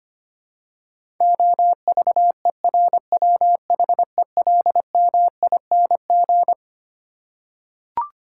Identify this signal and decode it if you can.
Morse code, heard as a keyed tone, about 25 wpm, 700 Hz. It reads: OVERWHELMING
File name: Twelve Letter Words - 25wpm